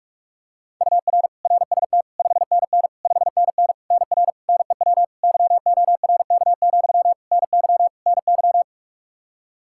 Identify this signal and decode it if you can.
Morse code, heard as a keyed tone, about 47 wpm, 700 Hz. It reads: UR RST 5NN 5NN NR NEW YORK, NY NY
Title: Practice QSO - Exchange 1 - Beginning - 45wpm